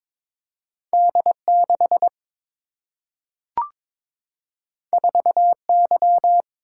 Morse code, heard as a keyed tone, about 21 wpm, 700 Hz. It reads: D 6 4 Y
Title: Single-Letter Single-Number 22wpm